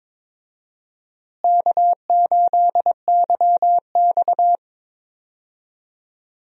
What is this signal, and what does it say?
Morse code, heard as a keyed tone, about 22 wpm, 700 Hz. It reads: K8YX
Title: USA Call Signs - 4 Characters - Repeated 2x - 22wpm